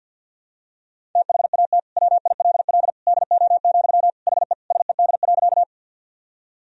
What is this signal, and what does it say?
Morse code, heard as a keyed tone, about 50 wpm, 700 Hz. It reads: THAT WILL DO, HE SED.
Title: Sentences from Top 100 Words and Abbreviations 50wpm